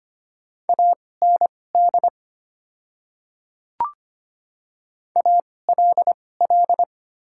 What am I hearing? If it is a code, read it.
Morse code, heard as a keyed tone, about 25 wpm, spaced as for 18 wpm, 700 Hz. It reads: AND ALL
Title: 2x Character Spacing - Top 100 Words - 25wpm